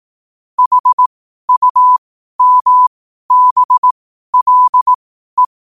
Morse code, heard as a keyed tone, about 18 wpm, spaced as for 12 wpm, 1000 Hz. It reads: HUMBLE
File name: qotdfiles / 12wpm